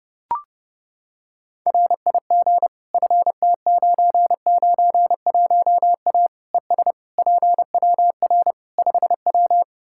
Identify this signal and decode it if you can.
Morse code, heard as a keyed tone, about 28 wpm, 700 Hz. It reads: RIG FT991A ES PWR 5W
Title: Practice QSO - Exchange 2 - Info - 30wpm